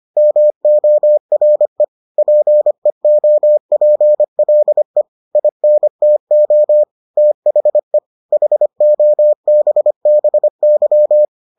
Morse code, heard as a keyed tone, about 25 wpm, 600 Hz. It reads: MORE PEOPLE INTO THE HOBBY